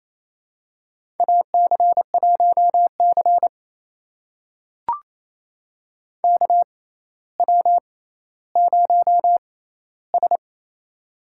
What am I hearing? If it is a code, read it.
Morse code, heard as a keyed tone, about 28 wpm, spaced as for 14 wpm, 700 Hz. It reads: AC1C KW0S
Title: USA Call Signs - 4 Characters - 6x Character Spacing - 28wpm